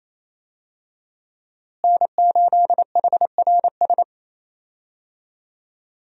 Morse code, heard as a keyed tone, about 28 wpm, 700 Hz. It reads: N8HRS
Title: USA Call Signs - 5 Characters - Repeated 3x - 28wpm